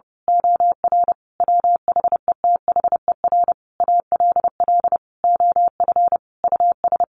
Morse code, heard as a keyed tone, about 30 wpm, 700 Hz. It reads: OR WHETHER ALL OF US
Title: A Sound Of Thunder 30wpm